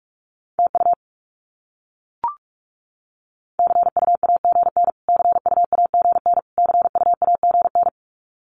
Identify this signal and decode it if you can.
Morse code, heard as a keyed tone, about 44 wpm, 700 Hz. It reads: TU CUAGN CUAGN CUAGN
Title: Abbreviations Repeated 3x - 45wpm